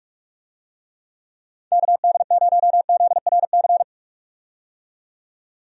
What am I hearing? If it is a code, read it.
Morse code, heard as a keyed tone, about 46 wpm, 700 Hz. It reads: KD0ZRC
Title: USA Call Signs - 6 Characters - 45wpm